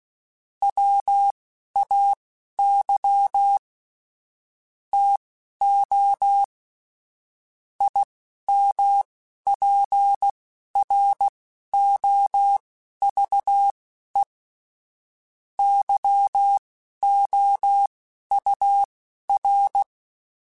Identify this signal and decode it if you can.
Morse code, heard as a keyed tone, about 16 wpm, spaced as for 11 wpm, 800 Hz. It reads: WAY TO IMPROVE YOUR